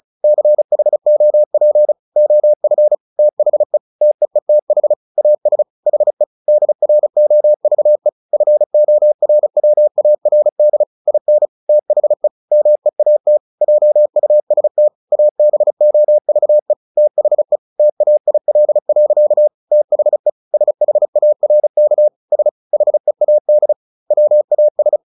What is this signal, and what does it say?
Morse code, heard as a keyed tone, about 34 wpm, 600 Hz. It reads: CHOP OF THE TEETH AS HE DROVE FORWARD IN THE MEAT JUST ABOVE THE TAIL. THE SHARK S HEAD WAS
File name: the old man and the sea 2 35wpm